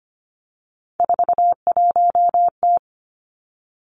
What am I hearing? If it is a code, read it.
Morse code, heard as a keyed tone, about 25 wpm, 700 Hz. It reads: 41T
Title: Three Challenging Letter-Number Combinations 25wpm